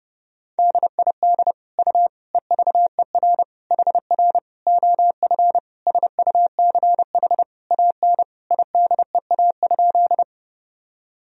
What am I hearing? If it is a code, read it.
Morse code, heard as a keyed tone, about 32 wpm, 700 Hz. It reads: DID U EVER HR OF SUCH AN IDEA?
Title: Sentences from Top 400 Words and Abbreviations 30wpm